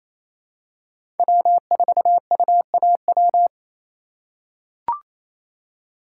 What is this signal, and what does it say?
Morse code, heard as a keyed tone, about 28 wpm, 700 Hz. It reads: W4UAW
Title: USA Call Signs - 5 Characters - 2x Character Spacing - 28wpm